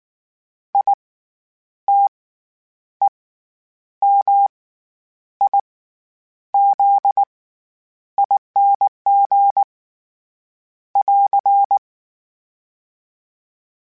Morse code, heard as a keyed tone, about 19 wpm, spaced as for 9 wpm, 800 Hz. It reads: ITEMIZING <AR>
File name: Special INGs / Special INGs-mp3s / SlowToFaster ing @18